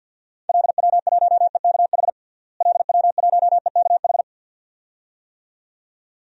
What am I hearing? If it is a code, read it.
Morse code, heard as a keyed tone, about 50 wpm, 700 Hz. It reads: LW1EKH LW1EKH
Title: DXCC-100 Call Signs - 6 Characters - Repeated 2x - 50wpm